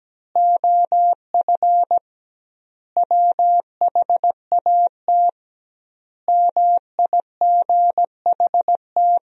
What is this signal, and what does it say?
Morse code, heard as a keyed tone, about 17 wpm, 700 Hz. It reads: OF WHAT MIGHT
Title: 2x Word-Spacing - Sentences from Top 400 Words - 17wpm